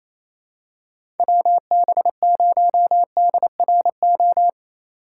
Morse code, heard as a keyed tone, about 28 wpm, 690 Hz. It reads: WB0DRO
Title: USA Call Signs - 6 Characters - 2x Character Spacing - 28wpm